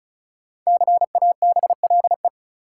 Morse code, heard as a keyed tone, about 36 wpm, 700 Hz. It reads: CABLE
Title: Five Letter Words - 35wpm